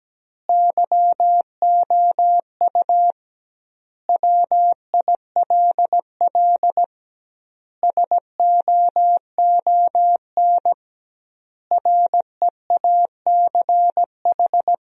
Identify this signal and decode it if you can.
Morse code, heard as a keyed tone, about 17 wpm, 700 Hz. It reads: YOU WILL SOON REACH
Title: 2x Word-Spacing - Sentences from Top 400 Words - 17wpm